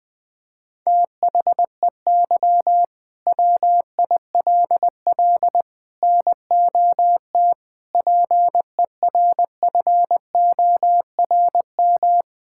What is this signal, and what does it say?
Morse code, heard as a keyed tone, about 20 wpm, 700 Hz. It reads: THEY WILL NOT PERFORM
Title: Sentences from Top 1000 Words - Review Part 5 - 20wpm